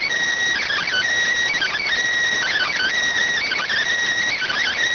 Emergency Alert System (SAME) tones, heard as harsh buzzing bursts. It is heard from 0.6 to 4.7 s.